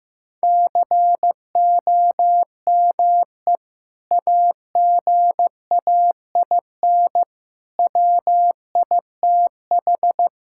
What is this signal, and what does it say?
Morse code, heard as a keyed tone, about 15 wpm, 700 Hz. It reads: COME AGAIN WITH
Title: Sentences from Top 1000 Words - Review Part 6 - 15wpm